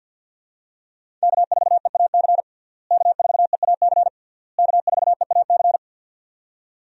Morse code, heard as a keyed tone, about 51 wpm, 700 Hz. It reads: K4EAC K4EAC K4EAC
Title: USA Call Signs - 5 Characters - Repeated 3x - 50wpm